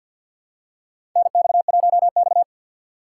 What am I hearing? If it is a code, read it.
Morse code, heard as a keyed tone, about 52 wpm, 700 Hz. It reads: NX1X
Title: USA Call Signs - 4 Characters - 2x Character Spacing - 50wpm